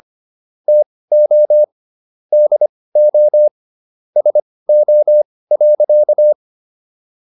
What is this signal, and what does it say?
Morse code, heard as a keyed tone, about 25 wpm, spaced as for 18 wpm, 600 Hz. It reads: TO DO SO.